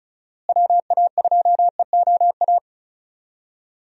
Morse code, heard as a keyed tone, about 34 wpm, 700 Hz. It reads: WA2EOA
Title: USA Call Signs - 6 Characters - Repeated 3x - 35wpm